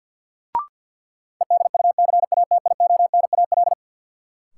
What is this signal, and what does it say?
Morse code, heard as a keyed tone, about 52 wpm, 700 Hz. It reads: EDUCATIONAL
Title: Sentences from Top 1200 Words - New Word Drill Alt - 50wpm